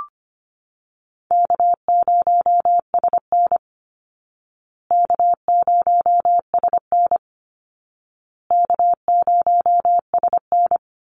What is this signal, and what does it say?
Morse code, heard as a keyed tone, about 25 wpm, 700 Hz. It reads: K0SN K0SN K0SN
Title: Easy Call Signs Repeated 3x - 25wpm